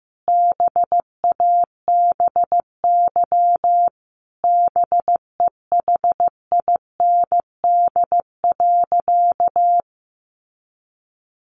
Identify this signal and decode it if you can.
Morse code, heard as a keyed tone, about 15 wpm, 700 Hz. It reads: BABY BEHIND.
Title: Sentences from Top 1300 Words 15wpm